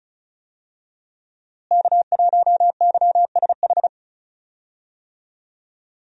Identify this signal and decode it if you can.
Morse code, heard as a keyed tone, about 34 wpm, 700 Hz. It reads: K1YSH
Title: USA Call Signs - 5 Characters - Repeated 2x - 35wpm